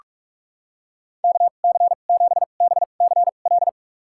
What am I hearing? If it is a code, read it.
Morse code, heard as a keyed tone, about 42 wpm, spaced as for 31 wpm, 700 Hz. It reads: KC7BCL
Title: USA Call Signs - 6 Characters - 2x Character Spacing - 45wpm